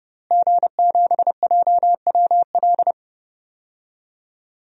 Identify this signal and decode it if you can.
Morse code, heard as a keyed tone, about 29 wpm, 700 Hz. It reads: G7JWL
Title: DXCC-100 Call Signs - 5 Characters - Repeated 3x - 30wpm